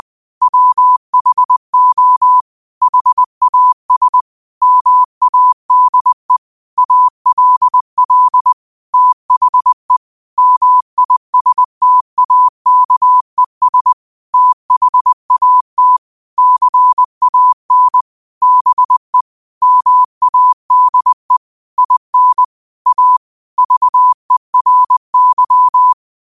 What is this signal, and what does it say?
Morse code, heard as a keyed tone, about 20 wpm, 1000 Hz. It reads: WHO HAS MADE ALL THE MISTAKES THAT CAN BE MADE IN A VERY
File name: qotdfiles / 20wpm